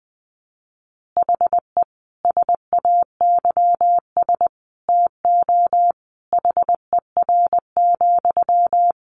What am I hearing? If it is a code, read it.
Morse code, heard as a keyed tone, about 20 wpm, 700 Hz. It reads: HE SAYS TO HER,
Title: Additional Easy Phrases 20wpm